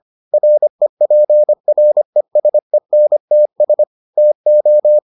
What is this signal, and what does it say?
Morse code, heard as a keyed tone, about 25 wpm, 600 Hz. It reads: REPRESENTS TO